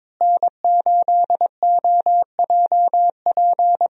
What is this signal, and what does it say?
Morse code, heard as a keyed tone, about 22 wpm, 700 Hz. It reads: N8OJP
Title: USA Call Signs - 5 Characters - Repeated 3x - 22wpm